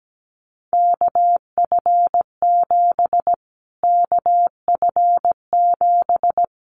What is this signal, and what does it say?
Morse code, heard as a keyed tone, about 17 wpm, 700 Hz. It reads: KF7 KF7
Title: Callsign Prefixes for Top 25 DXCC - Repeated 3x - 17wpm